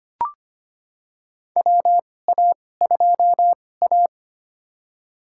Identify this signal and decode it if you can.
Morse code, heard as a keyed tone, about 25 wpm, spaced as for 18 wpm, 700 Hz. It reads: WA2A
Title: USA Call Signs - 2x Character Spacing - 25wpm